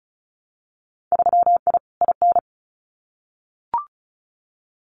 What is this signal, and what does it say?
Morse code, heard as a keyed tone, about 36 wpm, 700 Hz. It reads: 3I IN
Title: Winter Field Day Exchanges 35wpm